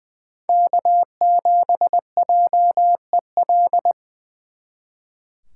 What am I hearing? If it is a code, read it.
Morse code, heard as a keyed tone, about 24 wpm, 700 Hz. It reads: K7JEL
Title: Speed Racing - USA Call Signs 20wpm